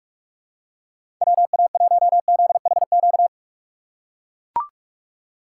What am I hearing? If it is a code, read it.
Morse code, heard as a keyed tone, about 44 wpm, 700 Hz. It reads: WA1ZHQ
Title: USA Call Signs - 6 Characters - 3x Character Spacing - 45wpm